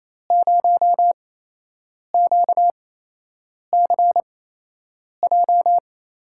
Morse code, heard as a keyed tone, about 28 wpm, spaced as for 8 wpm, 700 Hz. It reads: 0QCJ
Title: USA Call Signs - 5 Characters - 8x Character Spacing - 28wpm